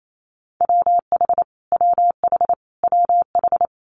Morse code, heard as a keyed tone, about 28 wpm, 700 Hz. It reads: WH WH WH
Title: Multi-letter Phonemes Repeated 3x 28wpm